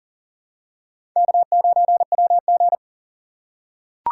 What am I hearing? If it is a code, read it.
Morse code, heard as a keyed tone, about 40 wpm, 700 Hz. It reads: K9WG
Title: USA Call Signs - 4 Characters - 3x Character Spacing - 40wpm